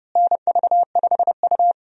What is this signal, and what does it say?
Morse code, heard as a keyed tone, about 31 wpm, 700 Hz. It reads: NV5U
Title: USA Call Signs - 4 Characters - Repeated 3x - 30wpm